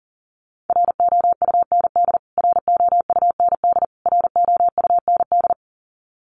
Morse code, heard as a keyed tone, about 40 wpm, 700 Hz. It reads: ROUND ROUND ROUND
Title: Top 500 Words Repeated 40wpm